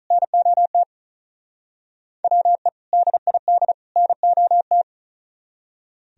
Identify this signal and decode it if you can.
Morse code, heard as a keyed tone, about 46 wpm, 700 Hz. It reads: NOT WE DID NOT